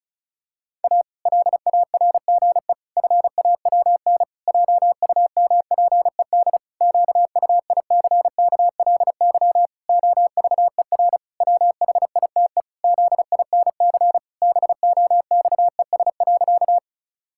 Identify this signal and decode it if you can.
Morse code, heard as a keyed tone, about 36 wpm, 700 Hz. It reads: A LARGE FAWN JUMPED QUICKLY OVER WHITE ZINC BOXES.
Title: Pangram Challenge v2 - 35wpm